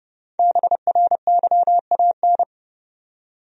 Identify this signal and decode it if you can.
Morse code, heard as a keyed tone, about 30 wpm, 700 Hz. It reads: BRYAN
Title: USA Common Names - Max 5 Letters - 30wpm